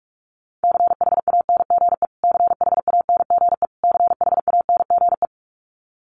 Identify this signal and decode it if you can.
Morse code, heard as a keyed tone, about 44 wpm, 700 Hz. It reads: CHANGE CHANGE CHANGE
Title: Top 200 Words Repeated 45wpm